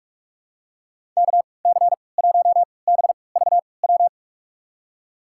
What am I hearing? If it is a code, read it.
Morse code, heard as a keyed tone, about 43 wpm, spaced as for 25 wpm, 700 Hz. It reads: KC1BVW
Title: USA Call Signs - 6 Characters - 3x Character Spacing - 45wpm